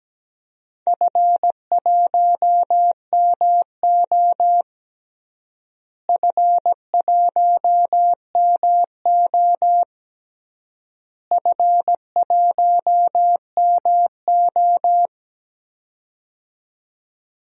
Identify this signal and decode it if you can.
Morse code, heard as a keyed tone, about 17 wpm, 700 Hz. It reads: F1MO F1MO F1MO
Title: DXCC-100 Call Signs - 4 Characters - Repeated 3x - 17wpm